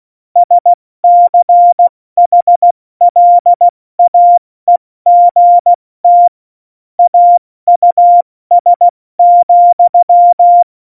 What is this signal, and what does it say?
Morse code, heard as a keyed tone, about 16 wpm, 700 Hz. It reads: SCHLAEGT AUS,
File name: Morsekurs / Witze / Witz 21 Juli 2020, 70 F80